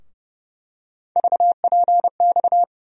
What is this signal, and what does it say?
Morse code, heard as a keyed tone, about 31 wpm, 700 Hz. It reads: VPX
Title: USA Call Sign Postfixes - 3 Letters 30wpm